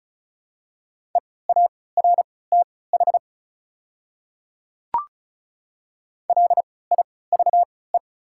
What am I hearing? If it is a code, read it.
Morse code, heard as a keyed tone, about 36 wpm, spaced as for 20 wpm, 700 Hz. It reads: EARTH LIVE
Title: 3x Character Spacing - Top 200 Words - 35wpm